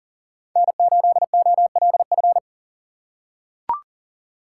Morse code, heard as a keyed tone, about 40 wpm, 700 Hz. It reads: N8OLF
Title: USA Call Signs - 5 Characters - 6x Character Spacing - 40wpm